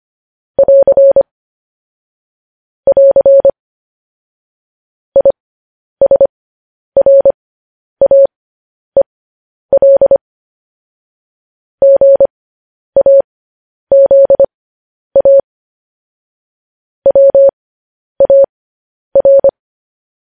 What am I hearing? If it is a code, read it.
Morse code, heard as a keyed tone, about 25 wpm, spaced as for 10 wpm, 550 Hz. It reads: <AR> <AR> ISRAEL GAZA WAR
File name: podcast / News headlines-1x-25x10